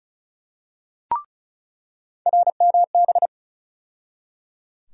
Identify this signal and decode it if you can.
Morse code, heard as a keyed tone, about 34 wpm, 700 Hz. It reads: RMB
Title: USA Call Sign Postfixes - 3 Letters 35wpm